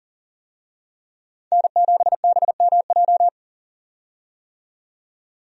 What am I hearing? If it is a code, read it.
Morse code, heard as a keyed tone, about 40 wpm, 700 Hz. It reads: N7BMJ
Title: USA Call Signs - 5 Characters - 40wpm